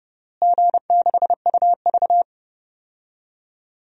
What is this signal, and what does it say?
Morse code, heard as a keyed tone, about 32 wpm, 700 Hz. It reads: G6UV
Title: DXCC-100 Call Signs - 4 Characters - Repeated 3x - 30wpm